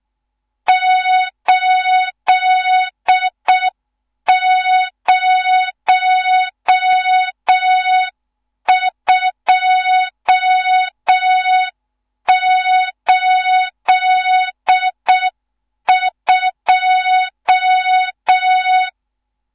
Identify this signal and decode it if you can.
Morse code, heard as a keyed tone, about 6 wpm, 3700 Hz, with noise. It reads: 80282
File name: Morze-HA2MN / hang / 14 280 HA2MN